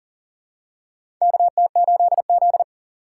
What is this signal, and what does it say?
Morse code, heard as a keyed tone, about 40 wpm, 700 Hz. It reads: KT8Z
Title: USA Call Signs - 4 Characters - 4x Character Spacing - 40wpm